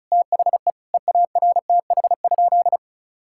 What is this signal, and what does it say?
Morse code, heard as a keyed tone, about 36 wpm, 700 Hz. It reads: THE EARTH?
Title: Sentences from Top 1000 Words - Review Part 5 - 35wpm